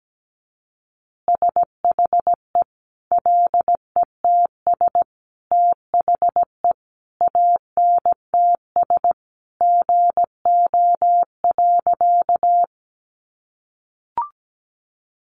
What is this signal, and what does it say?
Morse code, heard as a keyed tone, about 17 wpm, 700 Hz. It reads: SHE LETS THE ANTS GO.
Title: Easy Phrases 17wpm